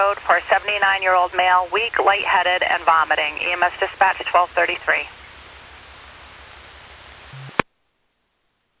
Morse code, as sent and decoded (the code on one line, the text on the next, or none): none